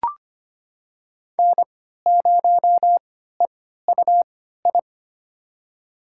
Morse code, heard as a keyed tone, about 25 wpm, spaced as for 14 wpm, 700 Hz. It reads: N0EUI
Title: USA Call Signs - 3x Character Spacing - 25wpm